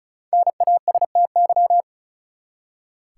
Morse code, heard as a keyed tone, about 35 wpm, 700 Hz. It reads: NASTY
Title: Five Letter Words - 35wpm